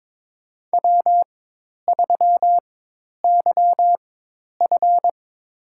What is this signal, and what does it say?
Morse code, heard as a keyed tone, about 22 wpm, spaced as for 10 wpm, 700 Hz. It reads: W3YF
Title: USA Call Signs - 4x Character Spacing - 22wpm